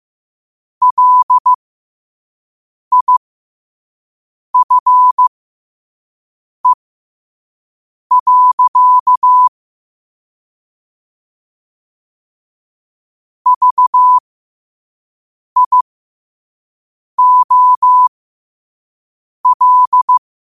Morse code, heard as a keyed tone, about 15 wpm, spaced as for 5 wpm, 1000 Hz. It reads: LIFE. VIOL
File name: qotdfiles / 5wpm